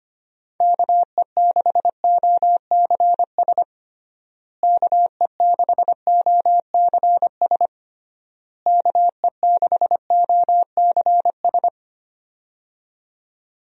Morse code, heard as a keyed tone, about 25 wpm, 700 Hz. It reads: KE6OCS KE6OCS KE6OCS
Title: USA Call Signs - 6 Characters - Repeated 3x - 25wpm